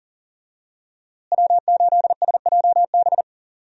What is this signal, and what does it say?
Morse code, heard as a keyed tone, about 40 wpm, 700 Hz. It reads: W8SJB
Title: USA Call Signs - 5 Characters - Repeated 3x - 40wpm